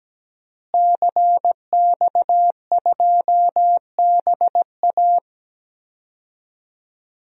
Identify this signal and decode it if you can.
Morse code, heard as a keyed tone, about 17 wpm, 700 Hz. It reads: CX2BA
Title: DXCC-100 Call Signs - 5 Characters - Repeated 2x - 17wpm